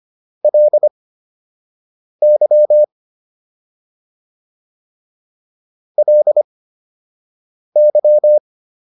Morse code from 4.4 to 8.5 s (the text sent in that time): LY